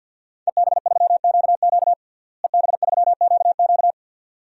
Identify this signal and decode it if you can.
Morse code, heard as a keyed tone, about 46 wpm, 700 Hz. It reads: EB3QQ EB3QQ
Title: DXCC-100 Call Signs - 5 Characters - Repeated 2x - 50wpm